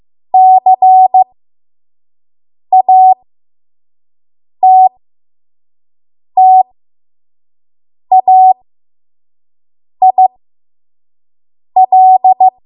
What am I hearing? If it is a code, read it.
Morse code, heard as a keyed tone, about 15 wpm, spaced as for 5 wpm, 750 Hz. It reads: CATTAIL
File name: Archive / 5 WPM / 05WPM